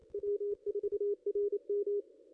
Morse code, heard as a keyed tone, about 28 wpm, 410 Hz. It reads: W4RM
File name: contests / ss-cw-k5tr-op-n5rz -0200